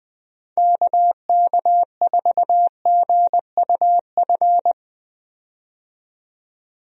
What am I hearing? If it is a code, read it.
Morse code, heard as a keyed tone, about 20 wpm, 700 Hz. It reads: KK4GUF
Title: USA Call Signs - 6 Characters - 20wpm